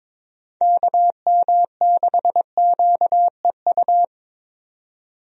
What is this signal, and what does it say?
Morse code, heard as a keyed tone, about 22 wpm, 700 Hz. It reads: KM6QEU
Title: USA Call Signs - 6 Characters - 22wpm